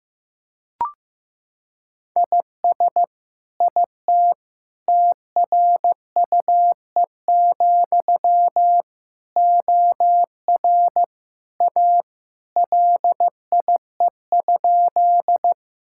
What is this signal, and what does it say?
Morse code, heard as a keyed tone, about 15 wpm, 700 Hz. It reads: IS IT TRUE, OR A LIE?
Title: Sentences from Top 1600 Words 15wpm